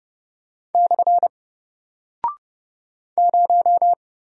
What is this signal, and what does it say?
Morse code, heard as a keyed tone, about 30 wpm, 700 Hz. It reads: / 0
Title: Single Character (Letter:Number:Punctuation:Prosign) Alt 30wpm